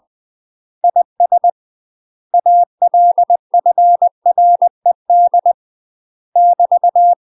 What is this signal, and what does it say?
Morse code, heard as a keyed tone, about 20 wpm, 700 Hz. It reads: IS ALFRED <BT>
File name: UK MASTER QSOs TRAINING / 20WPM / UK MASTER QSOs 0073